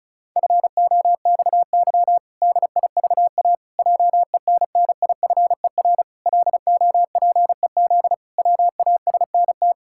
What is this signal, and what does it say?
Morse code, heard as a keyed tone, about 36 wpm, 700 Hz. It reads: FOXY DIVA JENNIFER LOPEZ WASNT
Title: Pangram Challenge v2 - 35wpm